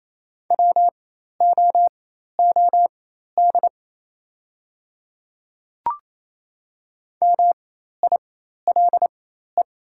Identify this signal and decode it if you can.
Morse code, heard as a keyed tone, about 28 wpm, spaced as for 13 wpm, 700 Hz. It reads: WOOD MILE
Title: 4x Character Spacing - Top 300 Words - 28wpm